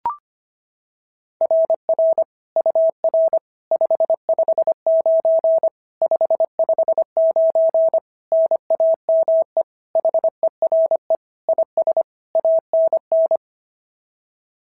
Morse code, heard as a keyed tone, about 25 wpm, 660 Hz. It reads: RR UR 559 559 NAME HERE IS ANN
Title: SOTA QSO Practice 25wpm